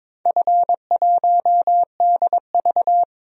Morse code, heard as a keyed tone, about 22 wpm, 700 Hz. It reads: F1DV
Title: DXCC-100 Call Signs - 4 Characters - Repeated 2x - 22wpm